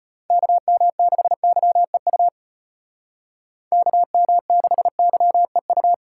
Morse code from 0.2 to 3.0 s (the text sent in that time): KM6YEU